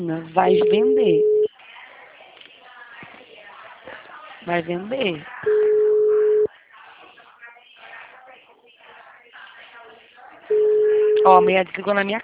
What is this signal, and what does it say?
Ringback tone (Europe, 425 Hz), with speech.